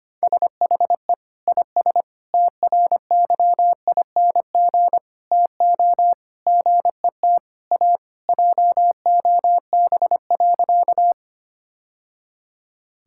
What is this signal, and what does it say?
Morse code, heard as a keyed tone, about 25 wpm, 700 Hz. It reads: SHE IS TRYING TO GET A JOB.
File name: practice / 25wpm / Sentences with 3-Word Phrases - 25wpm